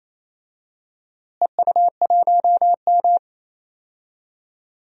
Morse code, heard as a keyed tone, about 27 wpm, 700 Hz. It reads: EU1M
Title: DXCC-100 Call Signs - 4 Characters - Repeated 2x - 28wpm